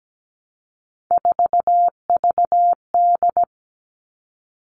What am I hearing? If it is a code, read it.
Morse code, heard as a keyed tone, about 17 wpm, 700 Hz. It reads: 4VD
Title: All Cut Numbers - 3-Digit 17wpm